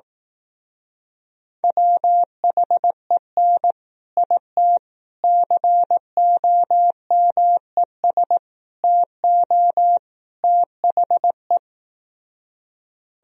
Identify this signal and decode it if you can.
Morse code, heard as a keyed tone, about 18 wpm, 700 Hz. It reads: WHEN IT COMES TO THE
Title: Speed-Racing - Sets of 5 Words 15wpm